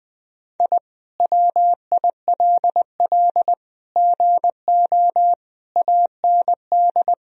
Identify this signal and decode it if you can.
Morse code, heard as a keyed tone, about 20 wpm, 700 Hz. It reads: I WILL GO AND
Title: Sentences from Top 1000 Words - Review Part 1 - 20wpm